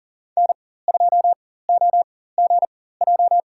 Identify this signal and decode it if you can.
Morse code, heard as a keyed tone, about 40 wpm, spaced as for 19 wpm, 700 Hz. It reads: N2OGJ
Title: USA Call Signs - 5 Characters - 4x Character Spacing - 40wpm